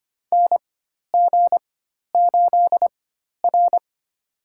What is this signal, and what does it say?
Morse code, heard as a keyed tone, about 25 wpm, spaced as for 12 wpm, 700 Hz. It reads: NG8R
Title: USA Call Signs - 4x Character Spacing - 25wpm